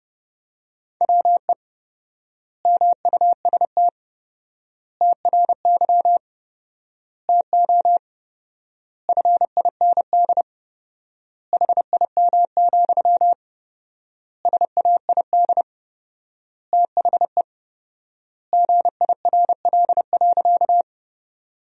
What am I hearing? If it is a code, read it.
Morse code, heard as a keyed tone, about 30 wpm, 700 Hz. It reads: WE MUST TRY TO FIND HIM, SAID THE GIRL.
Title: 4x Word-Spacing - Sentences from Top 400 Words - 30wpm